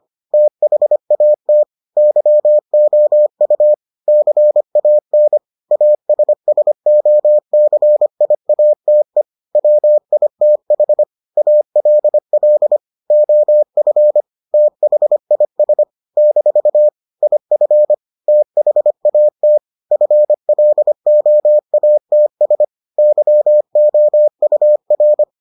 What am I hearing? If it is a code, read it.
Morse code, heard as a keyed tone, about 25 wpm, 600 Hz. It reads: THAT YOU CAN ASSOCIATE WITH ALL OF THIS - IF THAT FLOATS YOUR